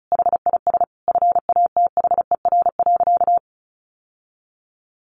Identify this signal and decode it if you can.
Morse code, heard as a keyed tone, about 34 wpm, 700 Hz. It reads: HIS FATHER.
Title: Sentences from Top 1200 Words 35wpm